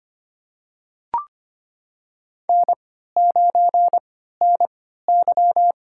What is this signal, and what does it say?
Morse code, heard as a keyed tone, about 25 wpm, spaced as for 14 wpm, 700 Hz. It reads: N9NY
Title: USA Call Signs - 3x Character Spacing - 25wpm